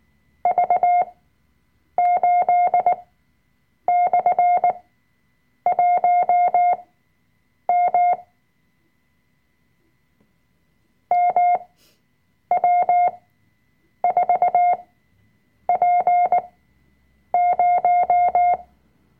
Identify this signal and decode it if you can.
Morse code, heard as a keyed tone, about 19 wpm, spaced as for 7 wpm, 710 Hz. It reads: V8/1M MW4P0
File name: Workshop Dateien / CW-Online-Kurs / Lektion 005